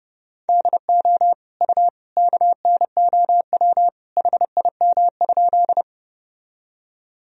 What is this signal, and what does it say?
Morse code, heard as a keyed tone, about 30 wpm, 700 Hz. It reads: DO U KNOW HIM?
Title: Sentences from Top 100 Words and Abbreviations 30wpm